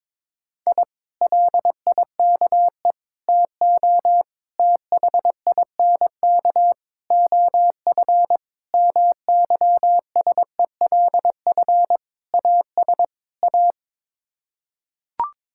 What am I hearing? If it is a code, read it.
Morse code, heard as a keyed tone, about 22 wpm, 700 Hz. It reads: I LIKE TO THINK OF MYSELF AS A
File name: practice / 22wpm / Sets of 8 Words - 22wpm